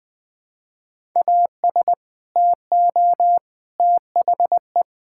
Morse code, heard as a keyed tone, about 20 wpm, 700 Hz. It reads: AS TO THE
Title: Sentences from Top 1000 Words - Review Part 5 - 20wpm